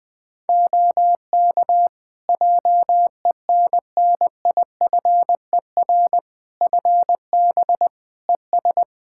Morse code, heard as a keyed tone, about 20 wpm, 700 Hz. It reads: OK JENNIFER FB ES
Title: Practice QSO - Exchange 2 - Info - 20wpm